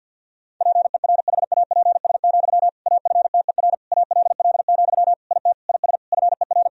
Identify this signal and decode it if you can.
Morse code, heard as a keyed tone, about 52 wpm, 700 Hz. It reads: PERHAPS, AFTER ALL, IT IS FER
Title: Sentences from Top 500 Words and Abbreviations 50wpm